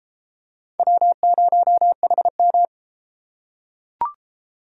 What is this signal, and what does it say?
Morse code, heard as a keyed tone, about 33 wpm, 700 Hz. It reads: W0HM